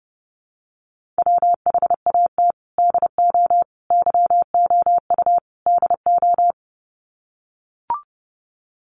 Morse code, heard as a keyed tone, about 30 wpm, 700 Hz. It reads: WHAT DO YOU DO
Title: Sets of 4 Words 30wpm